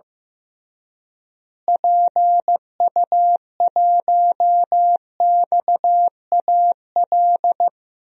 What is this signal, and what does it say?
Morse code, heard as a keyed tone, about 16 wpm, 700 Hz. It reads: PU1XAL
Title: DXCC-100 Call Signs - 6 Characters - Repeated 3x - 15wpm